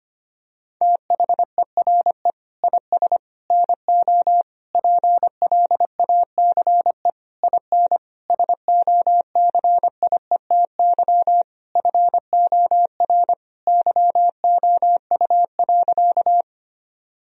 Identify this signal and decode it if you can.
Morse code, heard as a keyed tone, about 25 wpm, 700 Hz. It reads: THERE IS NO PLACE IN SOCIETY FOR YOU.
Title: Sentences from Top 1000 Words - Review Part 4 - 25wpm